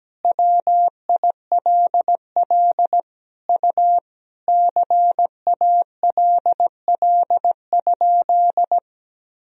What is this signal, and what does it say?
Morse code, heard as a keyed tone, about 17 wpm, 700 Hz. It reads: WILL U CALL?
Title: Sentences from Top 100 Words and Abbreviations 17wpm